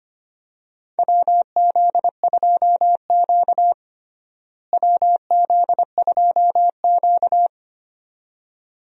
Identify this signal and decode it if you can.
Morse code, heard as a keyed tone, about 25 wpm, 700 Hz. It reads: WZ2Q WZ2Q
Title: USA Call Signs - 4 Characters - Repeated 2x - 25wpm